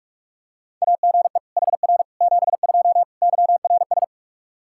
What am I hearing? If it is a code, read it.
Morse code, heard as a keyed tone, about 44 wpm, 700 Hz. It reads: AGE HR 72 YRS
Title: K1USN SST QSO Practice 45wpm